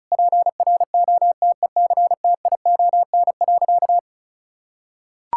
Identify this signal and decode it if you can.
Morse code, heard as a keyed tone, about 34 wpm, 700 Hz. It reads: PROTECTION.
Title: Sentences from Top 1600 Words 35wpm